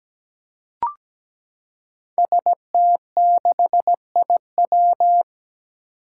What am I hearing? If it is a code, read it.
Morse code, heard as a keyed tone, about 17 wpm, 700 Hz. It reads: ST6IW
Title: Five Challenging Letter-Number Combinations 17wpm